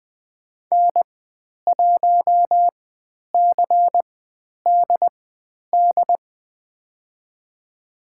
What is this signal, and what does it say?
Morse code, heard as a keyed tone, about 20 wpm, spaced as for 10 wpm, 700 Hz. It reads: N1CDD
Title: 095 Lesson 16g - Letter c - Call Signs 20wpm